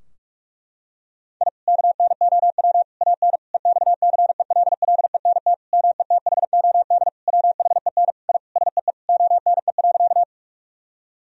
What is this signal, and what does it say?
Morse code, heard as a keyed tone, about 46 wpm, 700 Hz. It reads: I KNOW AN EXCELLENT METHOD WHEN I SEE ONE.
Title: Sentences from Top 1000 Words - Review Part 3 - 45wpm